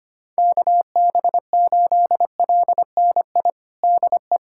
Morse code, heard as a keyed tone, about 25 wpm, 700 Hz. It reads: KB8LNI DE
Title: USA Callsign DE Callsign 25wpm